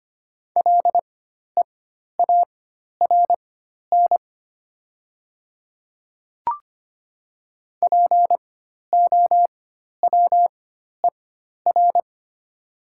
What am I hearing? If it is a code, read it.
Morse code, heard as a keyed tone, about 25 wpm, spaced as for 12 wpm, 700 Hz. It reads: LEARN POWER
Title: 4x Character Spacing - Top 500 Words - 25wpm